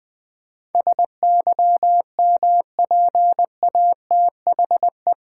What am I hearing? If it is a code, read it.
Morse code, heard as a keyed tone, about 20 wpm, 700 Hz. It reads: SYMPATHE
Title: Build A Word - 4 Letters At A Time - 11-Letter Words - 20wpm